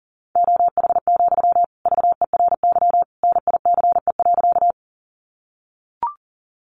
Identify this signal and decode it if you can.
Morse code, heard as a keyed tone, about 40 wpm, 700 Hz. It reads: OH, VERY NICE.
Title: Sentences from Top 1100 Words 40wpm